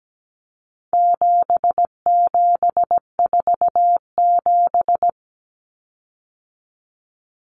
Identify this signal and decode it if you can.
Morse code, heard as a keyed tone, about 17 wpm, 700 Hz. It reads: 7747
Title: Common Cut Numbers - 4-Digit 17wpm